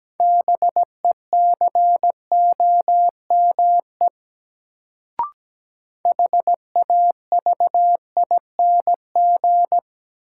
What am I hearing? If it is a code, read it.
Morse code, heard as a keyed tone, about 17 wpm, 700 Hz. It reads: BECOME HAVING
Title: Sets of 5 Words Encore - New Word Drill - 17wpm